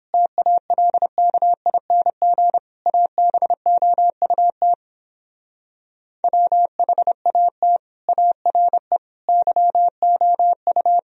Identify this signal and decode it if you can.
Morse code, heard as a keyed tone, about 31 wpm, 700 Hz. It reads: TALKING ABOUT WHAT ARE YOU